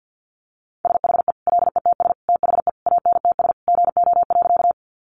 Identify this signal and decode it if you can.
Morse code, heard as a keyed tone, about 49 wpm, 700 Hz. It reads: SHE LETS THE ANTS GO.
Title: Additional Easy Phrases 50wpm